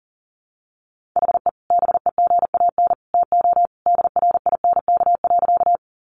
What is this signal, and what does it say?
Morse code, heard as a keyed tone, about 40 wpm, 700 Hz. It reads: HE BEGAN TO DRINK.
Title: Sentences from Top 1500 Words 40wpm